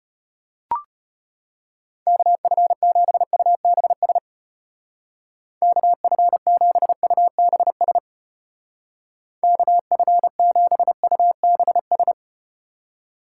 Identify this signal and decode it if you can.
Morse code, heard as a keyed tone, about 35 wpm, 700 Hz. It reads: KF7UBS KF7UBS KF7UBS